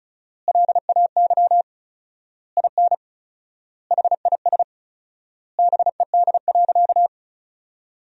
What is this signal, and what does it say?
Morse code, heard as a keyed tone, about 36 wpm, 700 Hz. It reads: LAY IN HIS BED.
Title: 4x Word-Spacing - Sentences from Top 500 Words - 35wpm